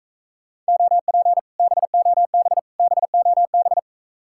Morse code, heard as a keyed tone, about 43 wpm, 700 Hz. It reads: OP BOB BOB